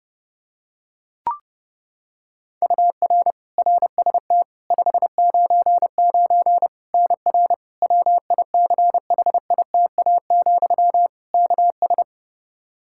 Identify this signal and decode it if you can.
Morse code, heard as a keyed tone, about 32 wpm, 700 Hz. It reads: UR RST 599 NR WICHITA, KS
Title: K1USN SST QSO Practice 30wpm